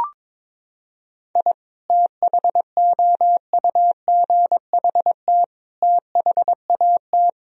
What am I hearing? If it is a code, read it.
Morse code, heard as a keyed tone, about 21 wpm, 700 Hz. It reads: I THOUGHT THAT
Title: Sentences from Top 1000 Words - Review Part 4 - 22wpm